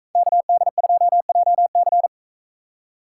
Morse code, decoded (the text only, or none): KD2JC